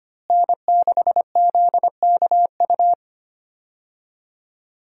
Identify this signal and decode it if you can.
Morse code, heard as a keyed tone, about 25 wpm, 700 Hz. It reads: N6ZKU
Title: USA Call Signs - 5 Characters - Repeated 2x - 25wpm